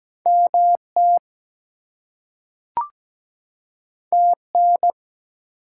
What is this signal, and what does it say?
Morse code, heard as a keyed tone, about 17 wpm, 700 Hz. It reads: MT TN
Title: US States and Canadian Provinces 17wpm